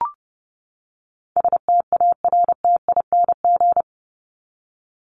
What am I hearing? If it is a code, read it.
Morse code, heard as a keyed tone, about 30 wpm, 700 Hz. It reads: STARTING
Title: ING Suffix - Top 100 Words with Base Words 30wpm